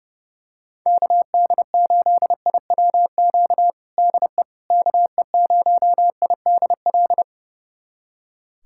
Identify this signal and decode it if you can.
Morse code, heard as a keyed tone, about 32 wpm, 700 Hz. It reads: KD8IWQ DE KE0IDL
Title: USA Callsign DE Callsign - 6 Characters - 30wpm